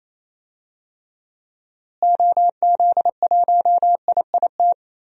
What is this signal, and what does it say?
Morse code, heard as a keyed tone, about 28 wpm, 700 Hz. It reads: OZ1IIT
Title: DXCC-100 Call Signs - 6 Characters - Repeated 3x - 28wpm